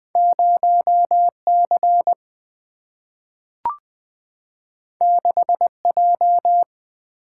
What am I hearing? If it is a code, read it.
Morse code, heard as a keyed tone, about 20 wpm, 700 Hz. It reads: 0C 6J
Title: Single-Letter Single-Number 20wpm